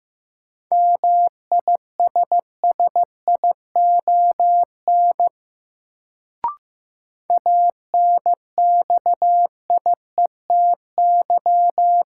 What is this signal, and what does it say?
Morse code, heard as a keyed tone, about 15 wpm, 700 Hz. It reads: MISSION ANXIETY
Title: Sentences from Top 1600 Words - New Word Drill Alt - 15wpm